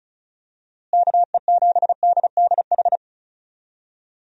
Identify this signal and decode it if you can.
Morse code, heard as a keyed tone, about 34 wpm, 700 Hz. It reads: KE7DDH
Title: USA Call Signs - 6 Characters - Repeated 3x - 35wpm